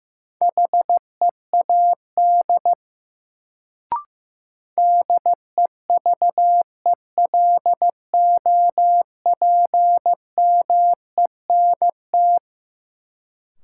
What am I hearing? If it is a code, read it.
Morse code, heard as a keyed tone, about 15 wpm, 700 Hz. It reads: HEAD DEVELOPMENT
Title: Sets of 3 Words - New Word Drill - 15wpm